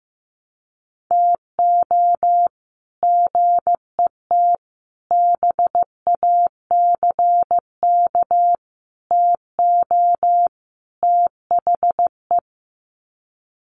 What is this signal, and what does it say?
Morse code, heard as a keyed tone, about 15 wpm, 700 Hz. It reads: TO GET BACK TO THE
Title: Sets of 5 Words Encore 15wpm